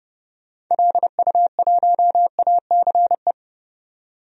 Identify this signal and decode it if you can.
Morse code, heard as a keyed tone, about 32 wpm, 700 Hz. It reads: LU1ACE
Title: DXCC-100 Call Signs - 6 Characters - 30wpm